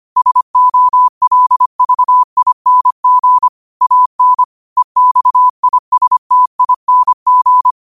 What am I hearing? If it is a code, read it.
Morse code, heard as a keyed tone, about 25 wpm, 1000 Hz. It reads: SOLVING AN EXISTING